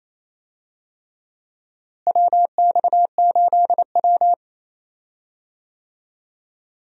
Morse code, heard as a keyed tone, about 28 wpm, 700 Hz. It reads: WX8W
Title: USA Call Signs - 4 Characters - Repeated 3x - 28wpm